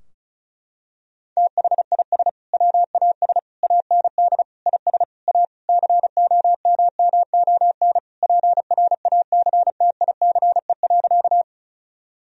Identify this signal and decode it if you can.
Morse code, heard as a keyed tone, about 36 wpm, 700 Hz. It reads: THIS WAS AND IS A COMMON PRACTICE.
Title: Sentences from Top 1000 Words - Review Part 4 - 35wpm